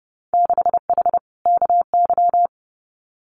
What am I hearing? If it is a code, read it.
Morse code, heard as a keyed tone, about 30 wpm, 700 Hz. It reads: 6H KY
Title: Winter Field Day Exchanges Encore 30wpm